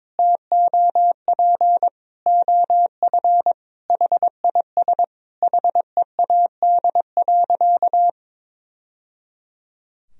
Morse code, heard as a keyed tone, about 22 wpm, 700 Hz. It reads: TOP OF HIS HEAD.
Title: Sentences from Top 1000 Words - Review Part 2 - 22wpm